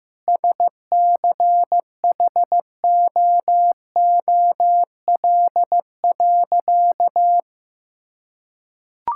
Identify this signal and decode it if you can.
Morse code, heard as a keyed tone, about 15 wpm, 700 Hz. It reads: SCHOOL.
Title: Sentences from Top 1000 Words - Review Part 5 - 15wpm